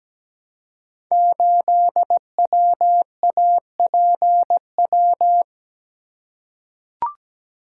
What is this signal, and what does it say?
Morse code, heard as a keyed tone, about 17 wpm, 700 Hz. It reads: 8WAPW
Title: Code Groups - 5 Letter:Numbers - 17wpm